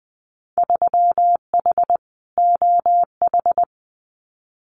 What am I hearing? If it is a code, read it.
Morse code, heard as a keyed tone, about 20 wpm, 700 Hz. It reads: 3H OH
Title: Winter Field Day Exchanges Encore 20wpm